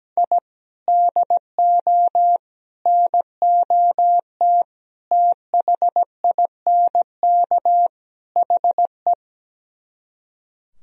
Morse code, heard as a keyed tone, about 17 wpm, 700 Hz. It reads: I DO NOT THINK HE
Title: Speed-Racing - Sets of 5 Words 17wpm